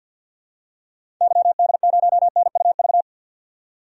Morse code, heard as a keyed tone, about 48 wpm, 700 Hz. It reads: YD0NUV
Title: DXCC-100 Call Signs - 6 Characters - 50wpm